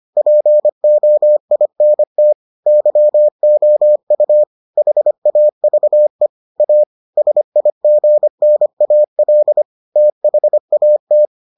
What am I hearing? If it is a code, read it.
Morse code, heard as a keyed tone, about 25 wpm, 600 Hz. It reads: POINT YOU HAVE A SIGNAL THAT